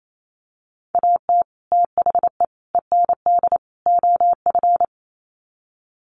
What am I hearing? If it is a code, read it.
Morse code, heard as a keyed tone, about 28 wpm, 700 Hz. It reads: AT THE END OF
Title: Sets of 4 Words 28wpm